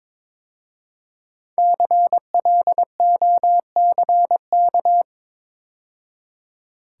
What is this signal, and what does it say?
Morse code, heard as a keyed tone, about 22 wpm, 700 Hz. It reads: CLOCK
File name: 5 Letter Words - First 3 Letters Given - 22wpm